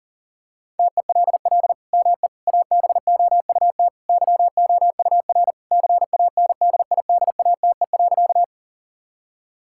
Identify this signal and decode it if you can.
Morse code, heard as a keyed tone, about 37 wpm, 700 Hz. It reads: TELL ME ABOUT YOUR CANDIDATE.
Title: Sentences from Top 1600 Words 40wpm